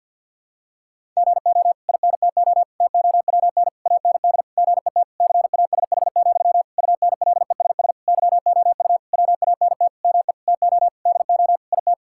Morse code, heard as a keyed tone, about 46 wpm, 700 Hz. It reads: GO INTO TOWN AND GET CASH, UNLESS YOU WANT ME TO DO IT
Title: Sentences from Top 1000 Words - Review Part 6 - 50wpm